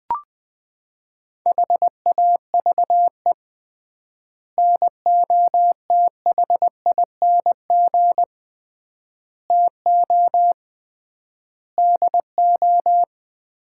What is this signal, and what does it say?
Morse code, heard as a keyed tone, about 21 wpm, 700 Hz. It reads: HAVE NOTHING TO DO
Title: 3x Word-Spacing - Sets of 5 Words - 20wpm